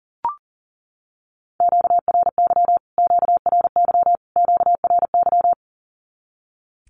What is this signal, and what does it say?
Morse code, heard as a keyed tone, about 40 wpm, 700 Hz. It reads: QRY QRY QRY
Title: Q-Codes Repeated 3x 40wpm